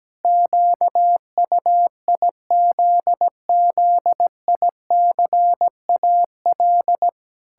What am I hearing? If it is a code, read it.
Morse code, heard as a keyed tone, about 17 wpm, 700 Hz. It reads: QUIZZICAL
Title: Pangram Challenge v2 - 17wpm